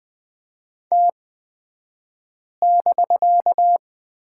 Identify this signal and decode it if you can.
Morse code, heard as a keyed tone, about 20 wpm, spaced as for 5 wpm, 700 Hz. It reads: T<BK>
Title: 249 Lesson 40d - BK - Two Characters 20wpm